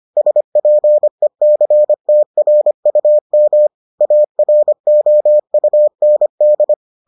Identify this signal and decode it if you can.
Morse code, heard as a keyed tone, about 25 wpm, 600 Hz. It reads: SPECTRUM AROUND